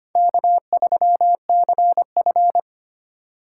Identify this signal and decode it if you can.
Morse code, heard as a keyed tone, about 25 wpm, 700 Hz. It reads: K3CF
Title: USA Call Signs - 4 Characters - 3x Character Spacing - 25wpm